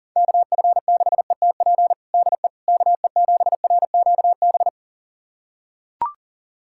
Morse code, heard as a keyed tone, about 40 wpm, 700 Hz. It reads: KF6ETP DE KE7RQB
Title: USA Callsign DE Callsign - 6 Characters - 40wpm